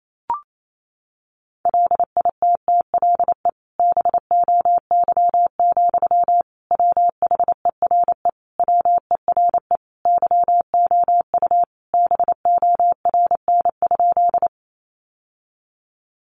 Morse code, heard as a keyed tone, about 28 wpm, 700 Hz. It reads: LITTLE BOY, WHERE WERE YOU BORN?
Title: Sentences from Top 1000 Words 28wpm